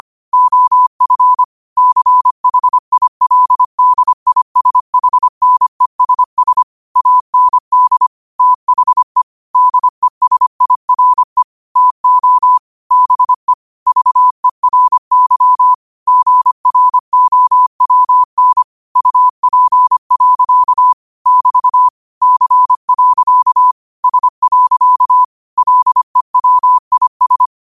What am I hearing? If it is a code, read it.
Morse code, heard as a keyed tone, about 25 wpm, 1000 Hz. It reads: OF CHILDISHNESS AND THE DESIRE TO BE VERY GROWN UP. <BT> C. S. LEWIS
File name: qotdfiles / 25wpm